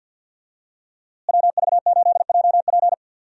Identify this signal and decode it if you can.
Morse code, heard as a keyed tone, about 46 wpm, 700 Hz. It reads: WV8JP
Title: USA Call Signs - 5 Characters - Repeated 3x - 50wpm